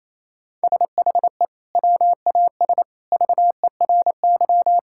Morse code, heard as a keyed tone, about 28 wpm, 700 Hz. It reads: SHE WAS VERY
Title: Sentences from Top 1600 Words 28wpm